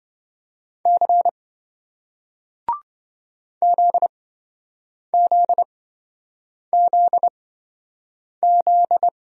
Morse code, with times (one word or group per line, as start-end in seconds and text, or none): 0.9-1.3 C
3.6-9.1 ZZZZ